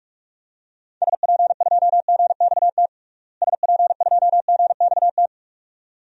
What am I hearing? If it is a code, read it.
Morse code, heard as a keyed tone, about 46 wpm, 700 Hz. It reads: SP2GXT SP2GXT
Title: DXCC-100 Call Signs - 6 Characters - Repeated 2x - 45wpm